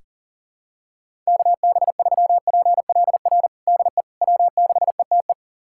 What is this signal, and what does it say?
Morse code, heard as a keyed tone, about 37 wpm, 700 Hz. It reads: KB3PLR DE W6ETE
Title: USA Callsign DE Callsign 40wpm